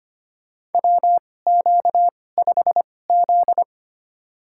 Morse code, heard as a keyed tone, about 25 wpm, spaced as for 18 wpm, 700 Hz. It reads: WQ5Z
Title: USA Call Signs - 4 Characters - 2x Character Spacing - 25wpm